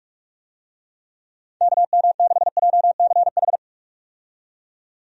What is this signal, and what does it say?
Morse code, heard as a keyed tone, about 41 wpm, 700 Hz. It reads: KM6JCH
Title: USA Call Signs - 6 Characters - 45wpm